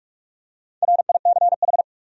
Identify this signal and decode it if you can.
Morse code, heard as a keyed tone, about 46 wpm, 700 Hz. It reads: RICH
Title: USA Common Names - 45wpm